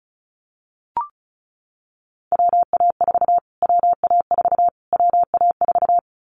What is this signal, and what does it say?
Morse code, heard as a keyed tone, about 34 wpm, 700 Hz. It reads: WA4 WA4 WA4
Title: Callsign Prefixes for Top 25 DXCC - Repeated 3x - 35wpm